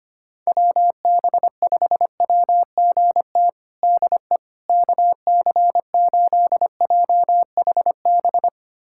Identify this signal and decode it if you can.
Morse code, heard as a keyed tone, about 25 wpm, 700 Hz. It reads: WB5WGT DE KC8JHB
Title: USA Callsign DE Callsign 25wpm